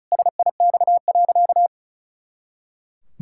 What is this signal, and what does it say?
Morse code, heard as a keyed tone, about 36 wpm, 700 Hz. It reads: SIX.
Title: Pangram Challenge v2 - 35wpm